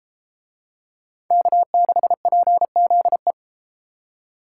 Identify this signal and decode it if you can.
Morse code, heard as a keyed tone, about 31 wpm, 700 Hz. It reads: K6PZE